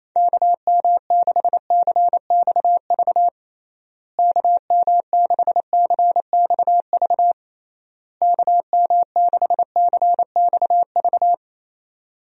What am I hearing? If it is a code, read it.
Morse code, heard as a keyed tone, about 27 wpm, 700 Hz. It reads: KM6CXV KM6CXV KM6CXV
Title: USA Call Signs - 6 Characters - Repeated 3x - 28wpm